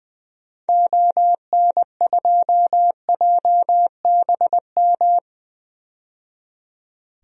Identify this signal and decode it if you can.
Morse code, heard as a keyed tone, about 21 wpm, 700 Hz. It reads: ON2JBM
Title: DXCC-100 Call Signs - 6 Characters - 20wpm